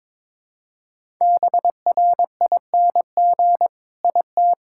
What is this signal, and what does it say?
Morse code, heard as a keyed tone, about 21 wpm, 700 Hz. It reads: BRING IT
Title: Sentences from Top 1000 Words - Review Part 5 - 22wpm